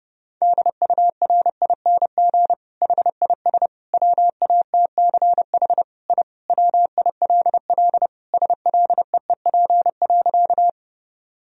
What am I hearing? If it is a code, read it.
Morse code, heard as a keyed tone, about 30 wpm, 700 Hz. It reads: DURING HIS WATCH I WILL SLEEP.
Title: Sentences from Top 1000 Words - Review Part 2 - 30wpm